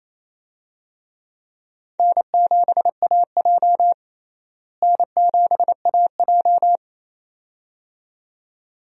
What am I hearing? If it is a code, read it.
Morse code, heard as a keyed tone, about 28 wpm, 700 Hz. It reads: N7AJ N7AJ
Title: USA Call Signs - 4 Characters - Repeated 2x - 28wpm